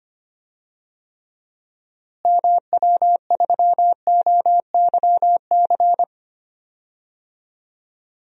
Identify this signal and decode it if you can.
Morse code, heard as a keyed tone, about 25 wpm, 700 Hz. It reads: MW3OYC
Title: DXCC-100 Call Signs - 6 Characters - Repeated 2x - 25wpm